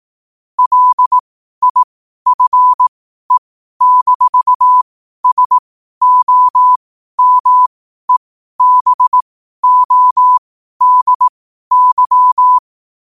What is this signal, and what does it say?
Morse code, heard as a keyed tone, about 18 wpm, spaced as for 12 wpm, 1000 Hz. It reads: LIFE-SOMEBODY
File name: qotdfiles / 12wpm